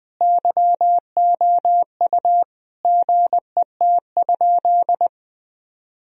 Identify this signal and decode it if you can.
Morse code, heard as a keyed tone, about 20 wpm, 700 Hz. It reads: YOU GET?
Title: Sentences from Top 1000 Words - Review Part 3 - 20wpm